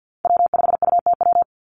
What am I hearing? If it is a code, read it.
Morse code, heard as a keyed tone, about 50 wpm, 700 Hz. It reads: W5UTW
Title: Call Signs - 50wpm